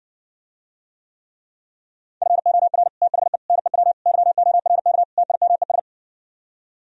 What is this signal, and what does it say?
Morse code, heard as a keyed tone, about 69 wpm, 700 Hz. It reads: FOR THE NEW YORK TIMES